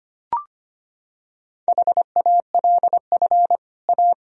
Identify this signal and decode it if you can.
Morse code, heard as a keyed tone, about 25 wpm, 700 Hz. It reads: HALF A
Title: Sentences from Top 1000 Words - Review Part 1 - 25wpm